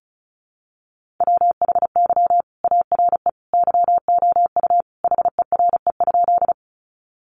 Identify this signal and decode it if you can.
Morse code, heard as a keyed tone, about 34 wpm, 700 Hz. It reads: WHY ARE YOU HERE?
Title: Sentences from Top 200 Words 35wpm